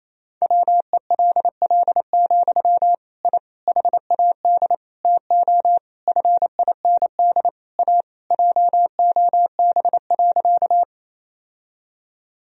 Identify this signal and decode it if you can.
Morse code, heard as a keyed tone, about 27 wpm, 700 Hz. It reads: WELL, I HAD TO FIND A JOB.
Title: Sentences from Top 1000 Words - Review Part 3 - 28wpm